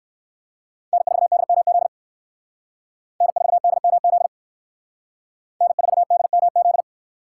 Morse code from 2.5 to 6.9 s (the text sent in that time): N4DMZ N4DMZ